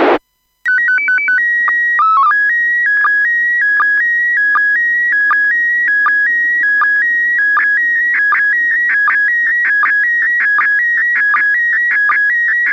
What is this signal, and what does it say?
SSTV (slow-scan television), PD 180.
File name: iss sstv 145mhz800 19h40 utc